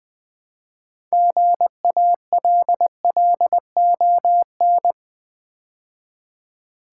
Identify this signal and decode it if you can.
Morse code, heard as a keyed tone, about 20 wpm, 700 Hz. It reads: GALLON
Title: Build A Word - 3 Letters At A Time - 6-Letter Words - 20wpm